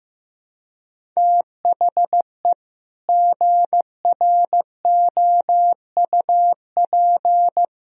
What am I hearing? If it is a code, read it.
Morse code, heard as a keyed tone, about 15 wpm, 700 Hz. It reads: THE GROUP
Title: Sentences from Top 1000 Words - Review Part 1 - 15wpm